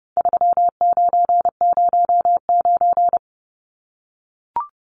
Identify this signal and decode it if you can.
Morse code, heard as a keyed tone, about 30 wpm, 700 Hz. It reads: 3909
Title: Numbers - 4-Digit 30wpm